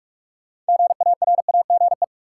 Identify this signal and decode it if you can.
Morse code, heard as a keyed tone, about 46 wpm, 700 Hz. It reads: GARAGE
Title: Sentences from Top 1600 Words - New Word Drill - 45wpm